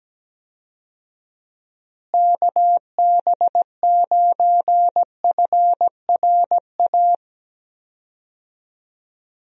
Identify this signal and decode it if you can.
Morse code, heard as a keyed tone, about 17 wpm, 700 Hz. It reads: KB9FRA
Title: USA Call Signs - 6 Characters - Repeated 3x - 17wpm